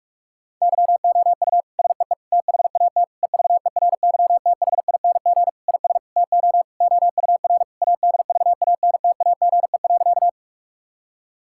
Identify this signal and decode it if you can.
Morse code, heard as a keyed tone, about 46 wpm, 700 Hz. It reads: YOU SEE THAT EVERYTHING IS TO OUR ADVANTAGE.
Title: Sentences from Top 1000 Words - Review Part 6 - 45wpm